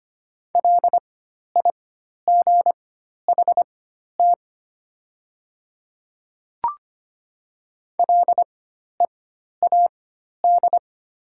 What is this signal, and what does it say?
Morse code, heard as a keyed tone, about 25 wpm, spaced as for 12 wpm, 700 Hz. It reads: LIGHT LEAD
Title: 4x Character Spacing - Top 500 Words - 25wpm